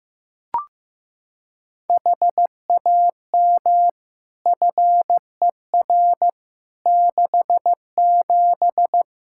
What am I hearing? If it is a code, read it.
Morse code, heard as a keyed tone, about 16 wpm, 700 Hz. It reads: HAM FER 67
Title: Practice QSO - Exchange 3 - More Info - 15wpm